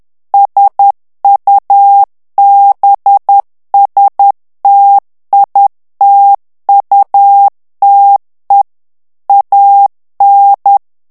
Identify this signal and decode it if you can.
Morse code, heard as a keyed tone, about 10 wpm, 800 Hz. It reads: SUBSTITUTE AN
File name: ANGLAIS texte clair 3 vitesse 10 mots minute